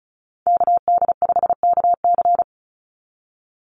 Morse code, heard as a keyed tone, about 36 wpm, 700 Hz. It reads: KD5KC
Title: NA SOTA Chaser Callsigns 35wpm